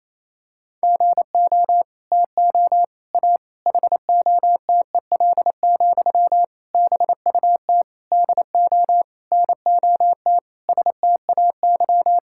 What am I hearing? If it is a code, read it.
Morse code, heard as a keyed tone, about 28 wpm, 700 Hz. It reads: GO TO A HOTEL, BUT DO NOT STAY
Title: Sentences from Top 1600 Words 28wpm